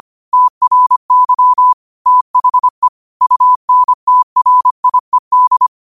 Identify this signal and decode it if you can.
Morse code, heard as a keyed tone, about 25 wpm, 1000 Hz. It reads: TRY THE UNTRIED
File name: qotdfiles / 25wpm